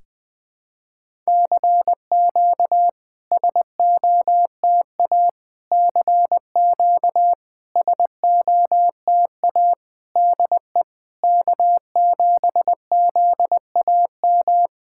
Morse code, heard as a keyed tone, about 20 wpm, 700 Hz. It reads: CQ SOTA CQ SOTA DE K7ZAM
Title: SOTA QSO Practice 20wpm